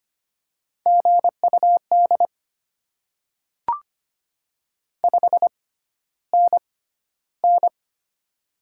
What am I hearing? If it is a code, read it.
Morse code, heard as a keyed tone, about 25 wpm, spaced as for 13 wpm, 700 Hz. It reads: GUD 5NN
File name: Top 20 QSO Elements - 6x Character Spacing - 25wpm